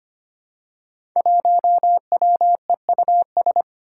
Morse code, heard as a keyed tone, about 25 wpm, 700 Hz. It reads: 1WEUS
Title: Five Challenging Letter-Number Combinations 25wpm